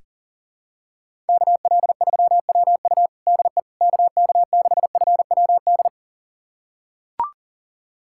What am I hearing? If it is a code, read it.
Morse code, heard as a keyed tone, about 40 wpm, 700 Hz. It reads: KL3WU DE KK6FWD
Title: USA Callsign DE Callsign 40wpm